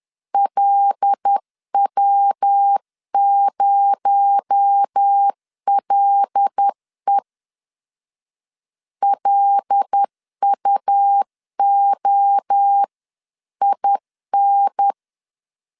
Morse code, heard as a keyed tone, about 10 wpm, 800 Hz. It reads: LW0LE LUOIN
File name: CW / DET / 1 6 1500C S C 10wpm 4252